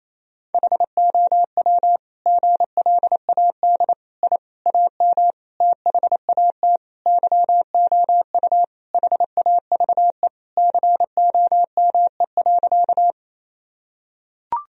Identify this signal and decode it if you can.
Morse code, heard as a keyed tone, about 28 wpm, 700 Hz. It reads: HOW GLAD I AM THAT YOU HAVE COME.
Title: Sentences from Top 1000 Words - Review Part 2 - 28wpm